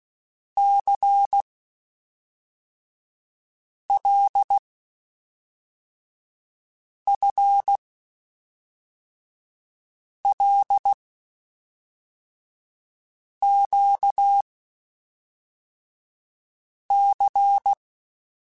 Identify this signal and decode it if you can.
Morse code, heard as a keyed tone, about 16 wpm, 800 Hz. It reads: C L F L Q C
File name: Basic Sets / Basic Letter Groups Collective7 / LFCYQ@5